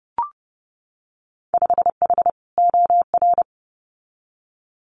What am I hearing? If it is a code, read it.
Morse code, heard as a keyed tone, about 30 wpm, 700 Hz. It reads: 5H OR
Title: Winter Field Day Exchanges Encore 30wpm